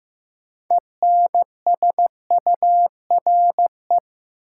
Morse code, heard as a keyed tone, about 15 wpm, 700 Hz. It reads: ENSURE
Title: Sentences from Top 1100 Words - New Word Drill Alt - 15wpm